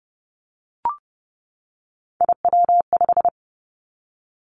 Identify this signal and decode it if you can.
Morse code, heard as a keyed tone, about 30 wpm, 700 Hz. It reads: IW5
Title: Three Challenging Letter-Number Combinations 30wpm